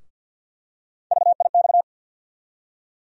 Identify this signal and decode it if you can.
Morse code, heard as a keyed tone, about 52 wpm, 700 Hz. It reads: VIX
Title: USA Call Sign Postfixes - 3 Letters 50wpm